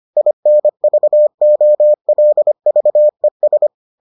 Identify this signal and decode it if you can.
Morse code, heard as a keyed tone, about 25 wpm, 600 Hz. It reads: INVOLVES